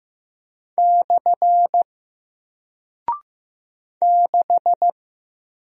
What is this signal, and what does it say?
Morse code, heard as a keyed tone, about 15 wpm, 700 Hz. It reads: / 6
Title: Single Character (Letter:Number:Punctuation:Prosign) Alt 15wpm